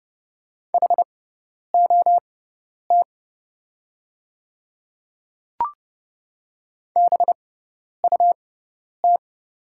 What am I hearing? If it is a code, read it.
Morse code, heard as a keyed tone, about 29 wpm, spaced as for 10 wpm, 690 Hz. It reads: HOT BUT
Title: 6x Character Spacing - Top 100 Words - 30wpm